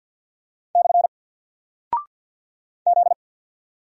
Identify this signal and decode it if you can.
Morse code, heard as a keyed tone, about 49 wpm, 700 Hz. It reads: / Z
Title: Single Character (Letter:Number:Punctuation:Prosign) 50wpm